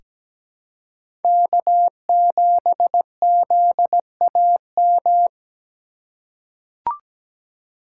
Morse code, heard as a keyed tone, about 17 wpm, 700 Hz. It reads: K7ZAM
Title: USA Call Signs - 5 Characters - 4x Character Spacing - 17wpm